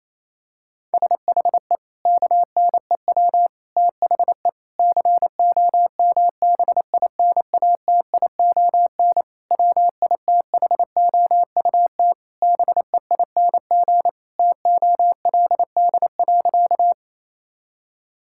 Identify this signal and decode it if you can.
Morse code, heard as a keyed tone, about 28 wpm, 700 Hz. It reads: SHE KNEW THE COMBINATION WITHOUT BEING TOLD.
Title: Sentences from Top 1000 Words - Review Part 6 - 28wpm